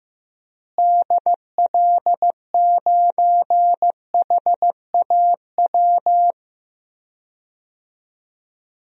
Morse code, heard as a keyed tone, about 15 wpm, 700 Hz. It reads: DL9HAW
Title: DXCC-100 Call Signs - 6 Characters - Repeated 3x - 15wpm